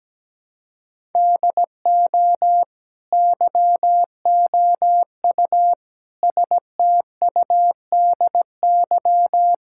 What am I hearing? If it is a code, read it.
Morse code, heard as a keyed tone, about 17 wpm, 700 Hz. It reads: DO YOU STUDY
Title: Sentences from Top 1000 Words - Review Part 6 - 17wpm